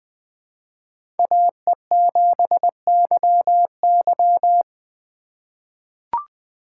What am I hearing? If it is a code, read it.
Morse code, heard as a keyed tone, about 20 wpm, 700 Hz. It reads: AE7YY
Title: USA Call Signs - 5 Characters - 6x Character Spacing - 20wpm